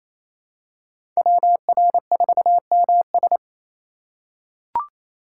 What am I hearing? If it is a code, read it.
Morse code, heard as a keyed tone, about 28 wpm, 700 Hz. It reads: WR4MS
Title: USA Call Signs - 5 Characters - 8x Character Spacing - 28wpm